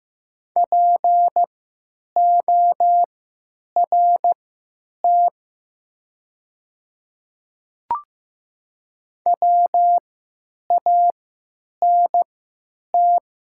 Morse code, heard as a keyed tone, about 15 wpm, spaced as for 9 wpm, 700 Hz. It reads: PORT WANT
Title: 3x Character Spacing - Top 500 Words - 15wpm